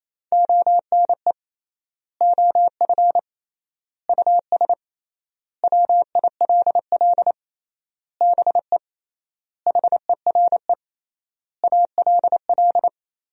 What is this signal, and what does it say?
Morse code, heard as a keyed tone, about 28 wpm, 700 Hz. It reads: ONE OF US WILL BE HERE ALL
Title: 3x Word-Spacing - Sentences from Top 300 Words - 28wpm